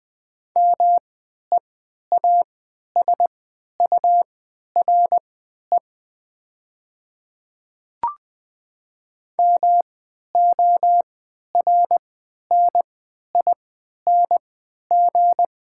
Morse code, heard as a keyed tone, about 20 wpm, spaced as for 11 wpm, 700 Hz. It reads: MEASURE MORNING
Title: 3x Character Spacing - Top 500 Words - 20wpm